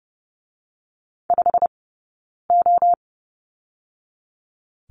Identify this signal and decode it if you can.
Morse code, heard as a keyed tone, about 30 wpm, spaced as for 9 wpm, 700 Hz. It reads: 5O
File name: Lesson 18d - Letter m - Two Characters 30wpm